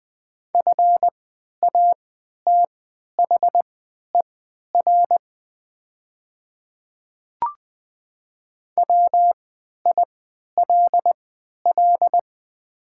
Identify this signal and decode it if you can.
Morse code, heard as a keyed tone, about 20 wpm, spaced as for 11 wpm, 700 Hz. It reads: FATHER WILL
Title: 3x Character Spacing - Top 300 Words - 20wpm